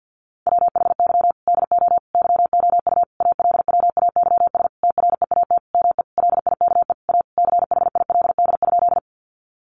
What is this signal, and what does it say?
Morse code, heard as a keyed tone, about 50 wpm, 700 Hz. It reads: WHY DO YOU ALWAYS TREAT ME LIKE A CHILD?
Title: Sentences from Top 1300 Words 50wpm